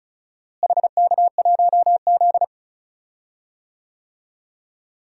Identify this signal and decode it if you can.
Morse code, heard as a keyed tone, about 36 wpm, 700 Hz. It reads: HK1Z
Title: DXCC-100 Call Signs - 4 Characters - Repeated 3x - 35wpm